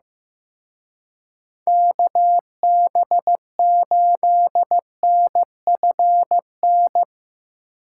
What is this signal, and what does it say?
Morse code, heard as a keyed tone, about 17 wpm, 700 Hz. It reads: KB8NFN
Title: Speed Racing - USA 2x3 Call Signs 15wpm